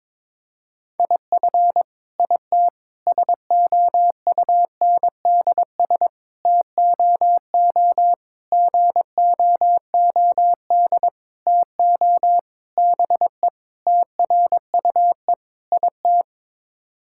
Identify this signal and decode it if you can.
Morse code, heard as a keyed tone, about 22 wpm, 700 Hz. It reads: IF IT SOUNDS TOO GOOD TO BE TRUE IT
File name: Sets of 9 Words - 22wpm